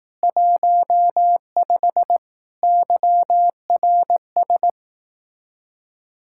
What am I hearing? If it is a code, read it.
Morse code, heard as a keyed tone, about 18 wpm, 700 Hz. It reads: 15 YRS
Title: Speed-Racing Common QSO Phrases Encore - 15wpm